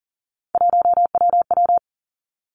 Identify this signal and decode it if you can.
Morse code, heard as a keyed tone, about 38 wpm, 700 Hz. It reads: 1WW
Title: Three Challenging Letter-Number Combinations 40wpm